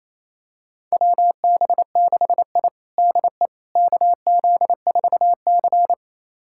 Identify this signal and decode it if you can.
Morse code, heard as a keyed tone, about 28 wpm, 700 Hz. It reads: WB6I DE KZ4C
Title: USA Callsign DE Callsign - 4 Characters - 28wpm